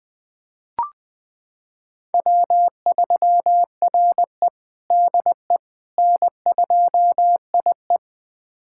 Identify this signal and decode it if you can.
Morse code, heard as a keyed tone, about 20 wpm, 700 Hz. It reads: W3RE DE N2IE
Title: USA Callsign DE Callsign - 4 Characters - 20wpm